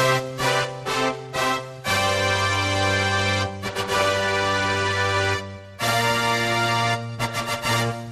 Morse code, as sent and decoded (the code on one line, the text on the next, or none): none